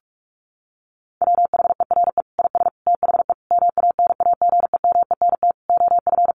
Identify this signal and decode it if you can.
Morse code, heard as a keyed tone, about 44 wpm, 700 Hz. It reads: WHERE IS THE MANAGEMENT OF
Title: Sentences from Top 1200 Words 45wpm